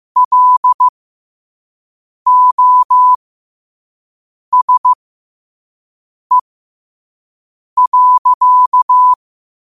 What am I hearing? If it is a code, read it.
Morse code, heard as a keyed tone, about 15 wpm, spaced as for 5 wpm, 1000 Hz. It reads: LOSE.
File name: qotdfiles / 5wpm